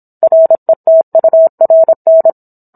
Morse code, heard as a keyed tone, about 27 wpm, 650 Hz. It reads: RETURN